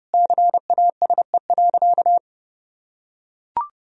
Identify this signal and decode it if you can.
Morse code, heard as a keyed tone, about 30 wpm, 700 Hz. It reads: CASE.
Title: Sentences from Top 1000 Words - Review Part 3 - 30wpm